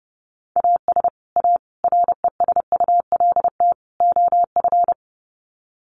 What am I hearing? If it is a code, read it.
Morse code, heard as a keyed tone, about 28 wpm, 700 Hz. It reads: AS A RESULT OF
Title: Sets of 4 Words 30wpm